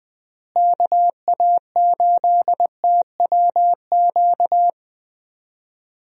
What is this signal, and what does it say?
Morse code, heard as a keyed tone, about 20 wpm, 700 Hz. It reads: KA8TWQ
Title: USA Call Signs - 6 Characters - Repeated 3x - 20wpm